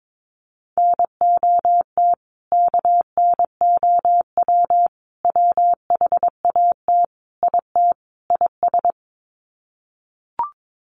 Morse code, heard as a keyed tone, about 22 wpm, 700 Hz. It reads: NOT KNOW WHAT IT IS
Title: Sets of 5 Words Encore 22wpm